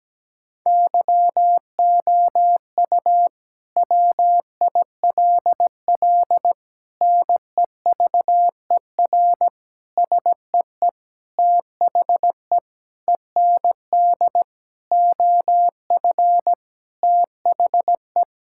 Morse code, heard as a keyed tone, about 17 wpm, 700 Hz. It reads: YOU WILL NEVER SEE THE END OF THE
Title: Sentences from Top 1600 Words 17wpm